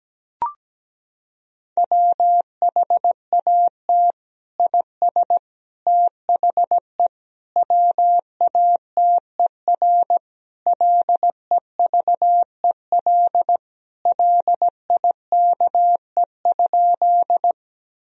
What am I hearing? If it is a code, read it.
Morse code, heard as a keyed tone, about 17 wpm, 700 Hz. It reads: WHAT IS THE WATER LEVEL LIKE?
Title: Sentences from Top 1000 Words - Review Part 4 - 17wpm